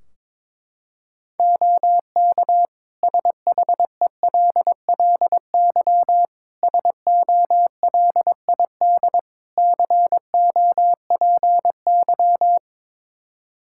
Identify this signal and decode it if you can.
Morse code, heard as a keyed tone, about 22 wpm, 700 Hz. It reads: OK SHELLY SOLID COPY
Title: Practice QSO - Exchange 3 - More Info - 22wpm